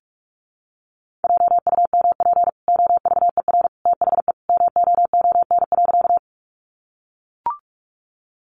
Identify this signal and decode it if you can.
Morse code, heard as a keyed tone, about 44 wpm, 700 Hz. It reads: JUMP OVER THE MOON.
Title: Sentences from Top 1300 Words 45wpm